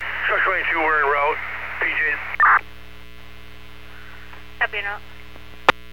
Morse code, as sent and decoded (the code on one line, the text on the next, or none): none